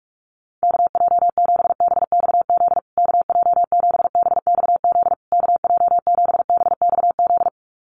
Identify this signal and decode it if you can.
Morse code, heard as a keyed tone, about 44 wpm, 700 Hz. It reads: KJ7BXZ KJ7BXZ KJ7BXZ
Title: USA Call Signs Repeated 3x 45wpm